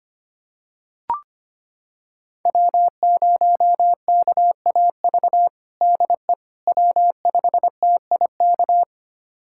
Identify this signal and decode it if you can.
Morse code, heard as a keyed tone, about 25 wpm, 700 Hz. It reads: W0KAV DE W5TIK
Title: USA Callsign DE Callsign - 5 Characters - 25wpm